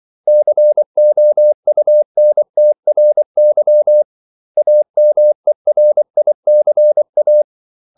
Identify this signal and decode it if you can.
Morse code, heard as a keyed tone, about 24 wpm, 600 Hz. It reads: COUNTRY AMERICA
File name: CW / Practice / 500MostCommonEnglishWords24